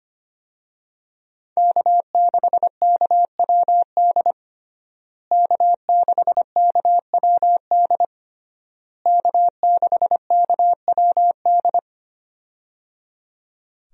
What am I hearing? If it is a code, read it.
Morse code, heard as a keyed tone, about 25 wpm, 700 Hz. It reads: K6KWD K6KWD K6KWD
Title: USA Call Signs - 5 Characters - Repeated 3x - 25wpm